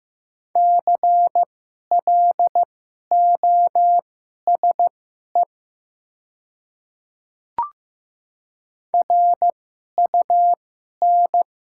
Morse code, heard as a keyed tone, about 15 wpm, spaced as for 11 wpm, 700 Hz. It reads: CLOSE RUN
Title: 2x Character Spacing - Top 300 Words - 15wpm